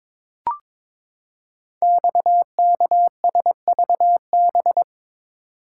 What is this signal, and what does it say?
Morse code, heard as a keyed tone, about 22 wpm, 700 Hz. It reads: XKSVB
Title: Code Groups - 5 Letters - 22wpm